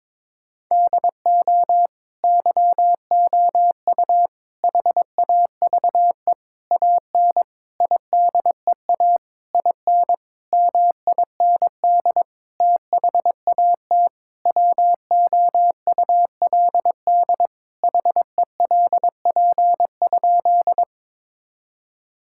Morse code, heard as a keyed tone, about 22 wpm, 700 Hz. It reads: DO YOU HAVE AN IDEA IN MIND THAT WOULD HELP?
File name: Sentences with 9-Word Phrases - 22wpm